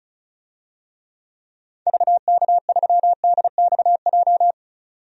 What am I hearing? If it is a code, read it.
Morse code, heard as a keyed tone, about 36 wpm, 700 Hz. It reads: VK3DXJ
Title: DXCC-100 Call Signs - 6 Characters - 35wpm